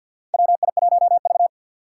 Morse code, heard as a keyed tone, about 50 wpm, 700 Hz. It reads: WI1V
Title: USA Call Signs - 2x Character Spacing - 50wpm